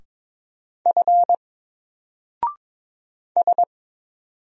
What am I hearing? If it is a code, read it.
Morse code, heard as a keyed tone, about 22 wpm, 700 Hz. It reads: F S
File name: Single-Letter-Number-22wpm